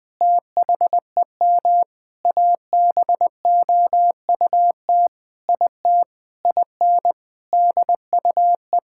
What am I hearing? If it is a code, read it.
Morse code, heard as a keyed tone, about 20 wpm, 700 Hz. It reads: THEM ABOUT IT IN DUE
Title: Sentences from Top 1000 Words - Review Part 6 - 20wpm